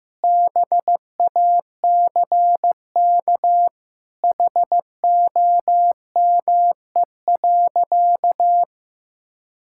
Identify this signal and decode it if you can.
Morse code, heard as a keyed tone, about 14 wpm, 700 Hz. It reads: BACK HOME.
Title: Sentences from Top 1000 Words - Review Part 6 - 15wpm